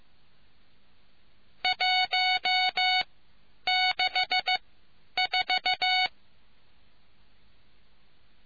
Morse code, heard as a keyed tone, about 15 wpm, spaced as for 9 wpm, 740 Hz. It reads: 164